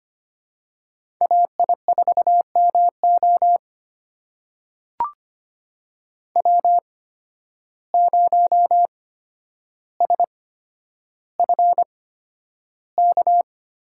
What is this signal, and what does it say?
Morse code, heard as a keyed tone, about 25 wpm, spaced as for 11 wpm, 700 Hz. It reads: AI4MO W0SFK
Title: USA Call Signs - 5 Characters - 8x Character Spacing - 25wpm